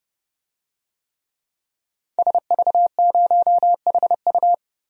Morse code, heard as a keyed tone, about 29 wpm, 700 Hz. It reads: SV0HU
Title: DXCC-100 Call Signs - 5 Characters - 30wpm